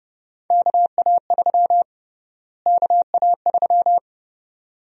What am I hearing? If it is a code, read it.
Morse code, heard as a keyed tone, about 30 wpm, 700 Hz. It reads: KA3 KA3
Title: USA Call Sign Prefixes - Repeated 2x - 30wpm